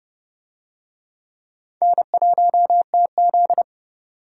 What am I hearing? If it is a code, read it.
Morse code, heard as a keyed tone, about 30 wpm, 700 Hz. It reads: N1TZ